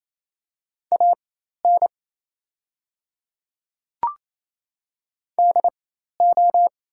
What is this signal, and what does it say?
Morse code, heard as a keyed tone, about 28 wpm, spaced as for 13 wpm, 690 Hz. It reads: AN DO
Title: 4x Character Spacing - Top 200 Words - 28wpm